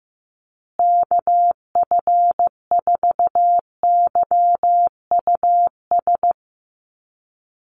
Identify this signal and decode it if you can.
Morse code, heard as a keyed tone, about 15 wpm, 700 Hz. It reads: KF4YUS
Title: USA Call Signs Repeated 3x 15wpm